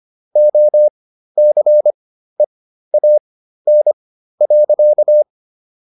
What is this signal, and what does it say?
Morse code, heard as a keyed tone, about 25 wpm, spaced as for 13 wpm, 600 Hz. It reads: OCEAN.